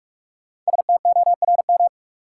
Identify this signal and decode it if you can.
Morse code, heard as a keyed tone, about 46 wpm, 700 Hz. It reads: STORM
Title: Sentences from Top 1500 Words - New Word Drill Alt - 45wpm